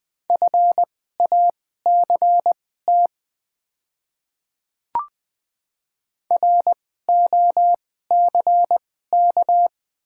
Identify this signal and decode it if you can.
Morse code, heard as a keyed tone, about 20 wpm, spaced as for 14 wpm, 690 Hz. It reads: FACT ROCK
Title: 2x Character Spacing - Top 500 Words - 20wpm